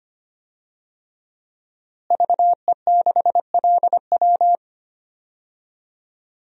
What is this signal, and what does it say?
Morse code, heard as a keyed tone, about 25 wpm, 700 Hz. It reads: VE6LW
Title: DXCC-100 Call Signs - 5 Characters - Repeated 3x - 25wpm